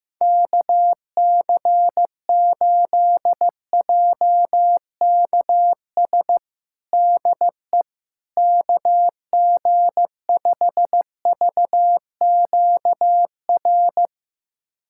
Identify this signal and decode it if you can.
Morse code, heard as a keyed tone, about 15 wpm, 700 Hz. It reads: KC8JKS DE KG5VQR
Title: USA Callsign DE Callsign - 6 Characters - 15wpm